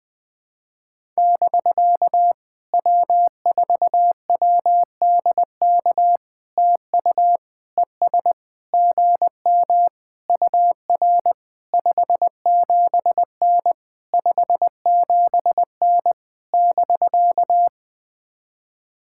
Morse code, heard as a keyed tone, about 20 wpm, 700 Hz. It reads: <BK> W4WDK TU ES GM UR 57N 57N <BK>
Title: POTA QSO Practice - 20wpm